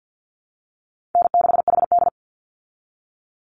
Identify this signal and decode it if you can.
Morse code, heard as a keyed tone, about 50 wpm, 700 Hz. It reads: N6HD
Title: Call Signs Repeated 3x - 50wpm